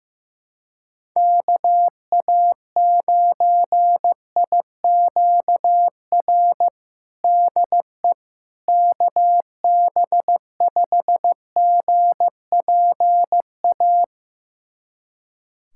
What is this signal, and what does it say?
Morse code, heard as a keyed tone, about 16 wpm, 700 Hz. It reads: KA9IQR DE KB5GPA
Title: USA Callsign DE Callsign - 6 Characters - 15wpm